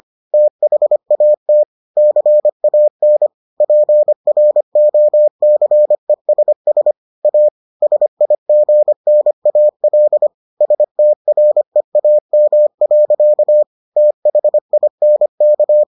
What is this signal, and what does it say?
Morse code, heard as a keyed tone, about 25 wpm, 600 Hz. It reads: THAT CAN PROCESS A SIGNAL STREAM. THINK